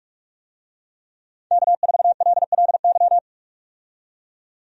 Morse code, heard as a keyed tone, about 46 wpm, 700 Hz. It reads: K4LLY
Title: USA Call Signs - 5 Characters - 45wpm